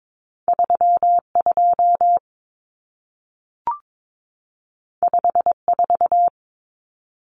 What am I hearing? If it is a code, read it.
Morse code, heard as a keyed tone, about 22 wpm, 700 Hz. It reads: 32 54
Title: Numbers - 2-Digit 22wpm